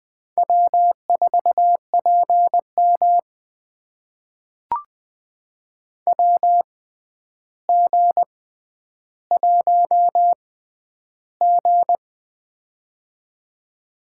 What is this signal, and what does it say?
Morse code, heard as a keyed tone, about 20 wpm, spaced as for 10 wpm, 700 Hz. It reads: W4PM WG1G
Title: USA Call Signs - 4 Characters - 6x Character Spacing - 20wpm